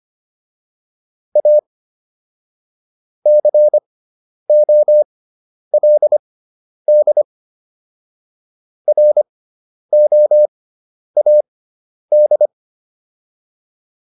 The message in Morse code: A COLD ROAD

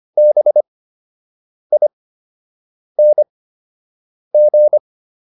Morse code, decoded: BING

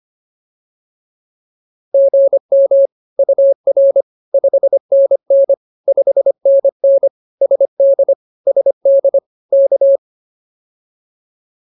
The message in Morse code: GM UR 5NN 5NN SD SD K

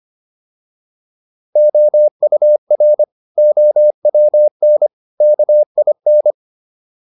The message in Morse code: OUR OWN KIN

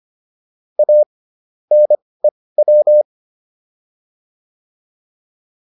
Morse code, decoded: A NEW

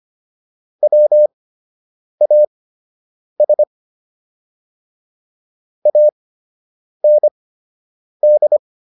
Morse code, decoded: WAS AND

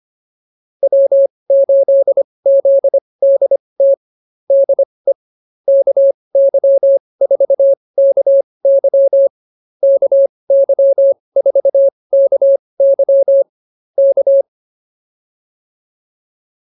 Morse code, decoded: W8ZDT DE KY4KY KY4KY K

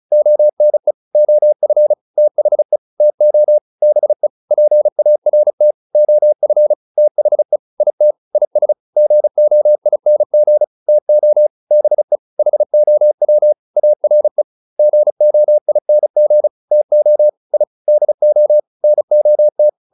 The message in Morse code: ONE OF THE TO BE PART OF THE IT IS GOING TO BE HOW ARE GOING TO I DO NOT

